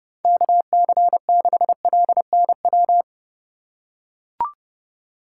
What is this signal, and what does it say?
Morse code, heard as a keyed tone, about 32 wpm, 700 Hz. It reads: KC6LNW
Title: USA Call Signs - 6 Characters - 3x Character Spacing - 30wpm